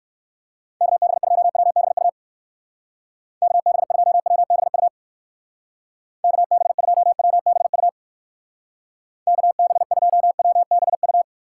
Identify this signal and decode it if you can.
Morse code, heard as a keyed tone, about 58 wpm, 700 Hz. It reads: KB2WBU KB2WBU KB2WBU KB2WBU